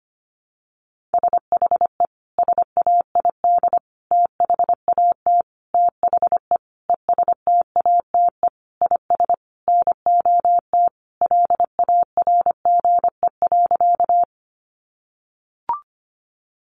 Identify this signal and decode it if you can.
Morse code, heard as a keyed tone, about 25 wpm, 700 Hz. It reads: SHE SAID THAT THE ESTATE IS NOT LARGE.
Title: Sentences from Top 1200 Words 25wpm